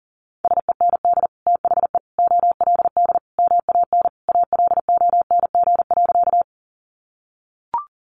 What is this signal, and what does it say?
Morse code, heard as a keyed tone, about 40 wpm, 700 Hz. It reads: SEND THE OLD MAN ALONG.
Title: Sentences from Top 1500 Words 40wpm